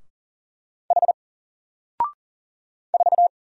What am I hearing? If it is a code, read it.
Morse code, heard as a keyed tone, about 38 wpm, 700 Hz. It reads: H 4
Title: Single Character (Letter:Number:Punctuation:Prosign) 40wpm